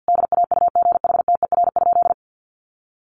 Morse code, heard as a keyed tone, about 50 wpm, 700 Hz. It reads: DAUGHTER?
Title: Sentences from Top 1500 Words 50wpm